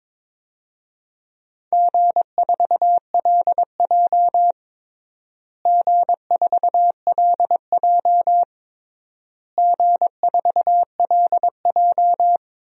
Morse code, heard as a keyed tone, about 22 wpm, 700 Hz. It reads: G4LJ G4LJ G4LJ
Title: DXCC-100 Call Signs - 4 Characters - Repeated 3x - 22wpm